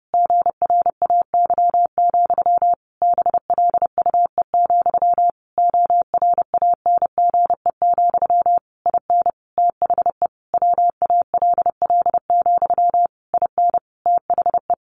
Morse code, heard as a keyed tone, about 30 wpm, 700 Hz. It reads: GRAY, BLUE, ORANGE, IN THE WALL, IN THE
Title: A Sound Of Thunder 30wpm